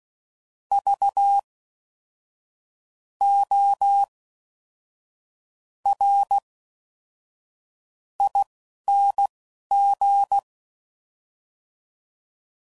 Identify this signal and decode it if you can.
Morse code, heard as a keyed tone, about 16 wpm, spaced as for 6 wpm, 800 Hz. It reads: VORING